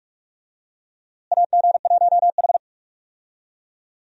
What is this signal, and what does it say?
Morse code, heard as a keyed tone, about 42 wpm, 700 Hz. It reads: AG1H
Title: USA Call Signs - 4 Characters - Repeated 2x - 45wpm